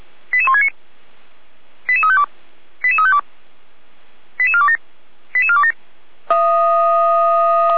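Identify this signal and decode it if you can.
Five-tone selective call (ZVEI), heard as quick runs of five tones, tones of about 70 ms. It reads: 88169 88352 88352 88427 88427
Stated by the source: Five-tone ZVEI selective call received on an SDR.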